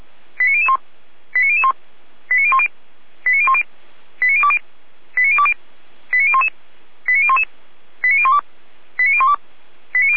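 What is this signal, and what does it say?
Five-tone selective call (ZVEI), heard as quick runs of five tones, tones of about 70 ms. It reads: 89001 89001 89010 89010 89020 89020 89011 89011 89012 89012
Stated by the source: ZVEI five-tone selective call received on an SDR, fire brigade Feuerwehr Orlamünde.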